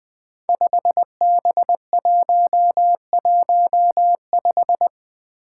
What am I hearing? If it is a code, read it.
Morse code, heard as a keyed tone, about 20 wpm, 700 Hz. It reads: 5B115
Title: Five Challenging Letter-Number Combinations 20wpm